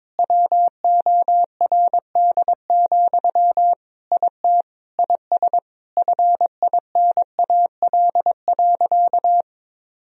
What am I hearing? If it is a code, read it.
Morse code, heard as a keyed tone, about 22 wpm, 700 Hz. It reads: WORD, IT IS FINAL.
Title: Sentences from Top 1000 Words - Review Part 2 - 22wpm